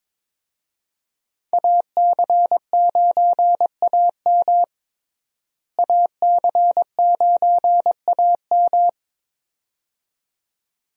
Morse code, heard as a keyed tone, about 22 wpm, 700 Hz. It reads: AC9AM AC9AM
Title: USA Call Signs - 5 Characters - Repeated 2x - 22wpm